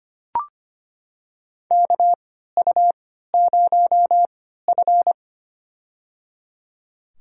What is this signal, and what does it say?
Morse code, heard as a keyed tone, about 25 wpm, spaced as for 14 wpm, 700 Hz. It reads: KU0F
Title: USA Call Signs - 3x Character Spacing - 25wpm